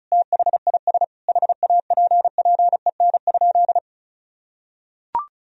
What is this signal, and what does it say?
Morse code, heard as a keyed tone, about 36 wpm, 700 Hz. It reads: THIS HAPPEN?
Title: Sentences from Top 400 Words and Abbreviations 35wpm